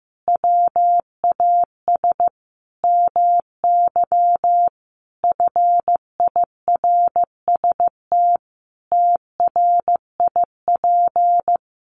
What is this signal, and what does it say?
Morse code, heard as a keyed tone, about 14 wpm, 700 Hz. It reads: WAS MY FIRST TRIP
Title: Sentences from Top 1300 Words 15wpm